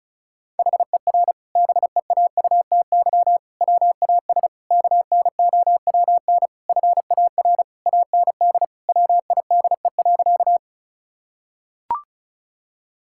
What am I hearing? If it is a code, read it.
Morse code, heard as a keyed tone, about 36 wpm, 700 Hz. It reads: HER BEAUTY WAS KNOWN FAR AND WIDE.
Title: Sentences from Top 1000 Words - Review Part 6 - 35wpm